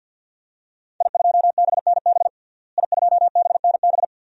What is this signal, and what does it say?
Morse code, heard as a keyed tone, about 51 wpm, 700 Hz. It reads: I2BNB I2BNB
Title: DXCC-100 Call Signs - 5 Characters - Repeated 2x - 50wpm